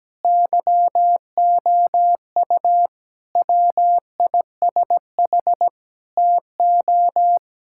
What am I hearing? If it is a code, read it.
Morse code, heard as a keyed tone, about 17 wpm, 700 Hz. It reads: YOU WISH TO
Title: Sentences from Top 1600 Words 17wpm